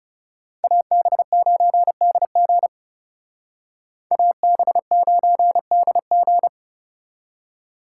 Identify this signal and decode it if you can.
Morse code, heard as a keyed tone, about 36 wpm, 700 Hz. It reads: AB9DG AB9DG
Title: Speed Racing - USA 2x2 Call Signs 30wpm